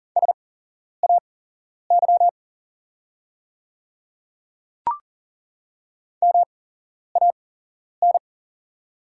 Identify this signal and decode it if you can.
Morse code, heard as a keyed tone, about 40 wpm, spaced as for 11 wpm, 700 Hz. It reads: SAY MAN
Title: 8x Character Spacing - Top 300 Words - 40wpm